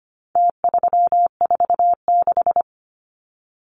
Morse code, heard as a keyed tone, about 25 wpm, 700 Hz. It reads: T346
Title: Numbers - Contest Style - 4-Digit 25wpm